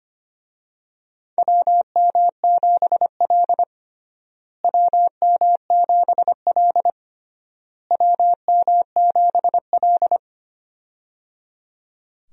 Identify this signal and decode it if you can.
Morse code, heard as a keyed tone, about 25 wpm, 700 Hz. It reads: WM7L WM7L WM7L
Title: USA Call Signs - 4 Characters - Repeated 3x - 25wpm